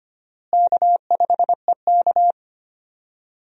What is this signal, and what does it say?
Morse code, heard as a keyed tone, about 25 wpm, 700 Hz. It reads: K5EK
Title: USA Call Signs - 4 Characters - Repeated 3x - 25wpm